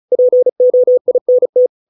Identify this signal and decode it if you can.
Morse code, heard as a keyed tone, about 34 wpm, 500 Hz. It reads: POINT